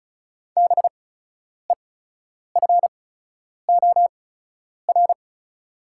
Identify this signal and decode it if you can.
Morse code, heard as a keyed tone, about 36 wpm, spaced as for 10 wpm, 690 Hz. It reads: BEFOR
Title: 8x Character Spacing - Top 200 Words - 35wpm